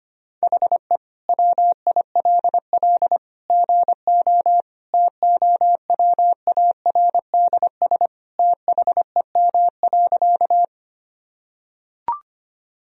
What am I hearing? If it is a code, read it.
Morse code, heard as a keyed tone, about 25 wpm, 700 Hz. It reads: HE WILL GO TOWARDS THEM.
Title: Sentences from Top 1000 Words - Review Part 2 - 25wpm